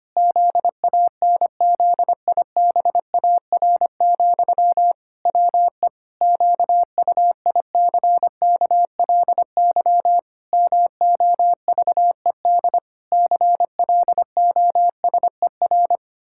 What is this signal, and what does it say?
Morse code, heard as a keyed tone, about 25 wpm, 700 Hz. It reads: ZANZIBAR, WE QUICKLY MOVED CLOSER
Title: Pangram Challenge v2 - 25wpm